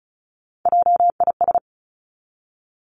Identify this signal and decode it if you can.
Morse code, heard as a keyed tone, about 36 wpm, 700 Hz. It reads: JIS
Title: Three Challenging Letter-Number Combinations 35wpm